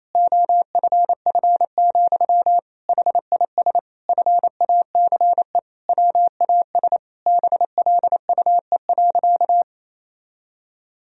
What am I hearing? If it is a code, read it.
Morse code, heard as a keyed tone, about 28 wpm, 700 Hz. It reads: OFF, HIS FACE WAS BLUE.
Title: Sentences from Top 1000 Words - Review Part 3 - 28wpm